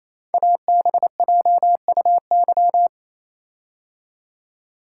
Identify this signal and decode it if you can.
Morse code, heard as a keyed tone, about 28 wpm, 700 Hz. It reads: ABJUY
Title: Code Groups - 5 Letters - 28wpm